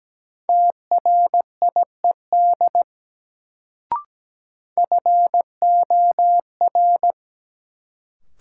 Morse code, heard as a keyed tone, about 17 wpm, 700 Hz. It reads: TRIED FOR
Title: Sets of 2 Words - New Word Drill - 17wpm